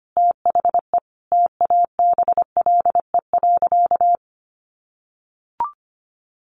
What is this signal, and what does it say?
Morse code, heard as a keyed tone, about 25 wpm, 700 Hz. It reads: THE TABLE.
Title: Sentences from Top 1300 Words 25wpm